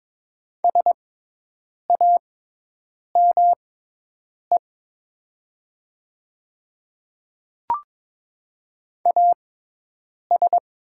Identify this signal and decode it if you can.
Morse code, heard as a keyed tone, about 23 wpm, spaced as for 8 wpm, 700 Hz. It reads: SAME AS
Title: 6x Character Spacing - Top 200 Words - 22wpm